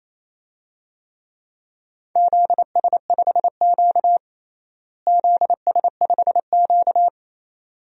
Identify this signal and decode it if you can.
Morse code, heard as a keyed tone, about 27 wpm, 700 Hz. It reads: ZS5Q ZS5Q
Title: DXCC-100 Call Signs - 4 Characters - Repeated 2x - 28wpm